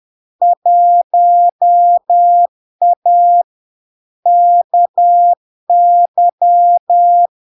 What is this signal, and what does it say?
Morse code, heard as a keyed tone, about 10 wpm, 700 Hz. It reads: 1A KY
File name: LARK-FD-2022-Log-in-CW-10wpm-700-Hz